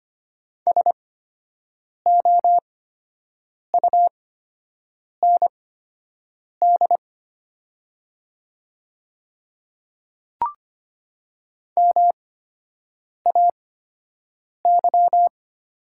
Morse code, heard as a keyed tone, about 25 wpm, spaced as for 7 wpm, 700 Hz. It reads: SOUND MAY
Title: 8x Character Spacing - Top 200 Words - 25wpm